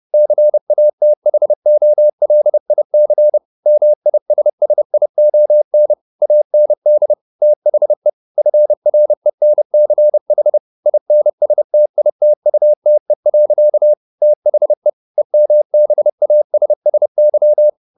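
Morse code, heard as a keyed tone, about 30 wpm, 600 Hz. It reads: CATHOLIC MISSION AND THE FRENCH INSTITUTE. THE EMBASSY
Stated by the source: Morse code at 29 wpm, sending Ken Follett's novel Never in English, from CW med Gnister.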